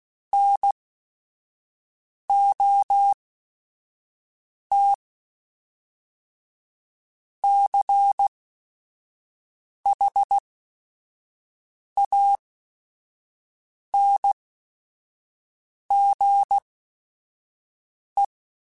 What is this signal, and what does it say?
Morse code, heard as a keyed tone, about 16 wpm, spaced as for 5 wpm, 800 Hz. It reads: NOT CHANGE
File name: Quotes / Quotes-mp3s / IgnoringFacts @5